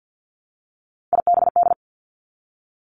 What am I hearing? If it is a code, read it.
Morse code, heard as a keyed tone, about 50 wpm, 700 Hz. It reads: IBD
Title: Three Challenging Letter-Number Combinations 50wpm